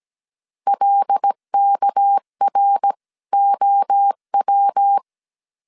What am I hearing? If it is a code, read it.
Morse code, heard as a keyed tone, about 17 wpm, 800 Hz. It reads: LKROW
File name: 16wpm